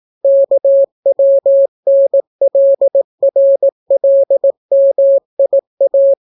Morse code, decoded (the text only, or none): KWNLRLMIA